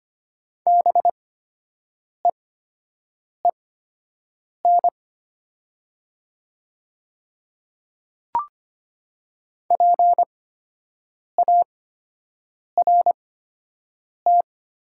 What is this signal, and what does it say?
Morse code, heard as a keyed tone, about 25 wpm, spaced as for 7 wpm, 700 Hz. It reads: BEEN PART
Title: 8x Character Spacing - Top 200 Words - 25wpm